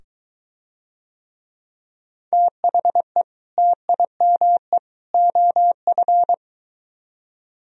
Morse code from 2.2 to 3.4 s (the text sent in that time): THE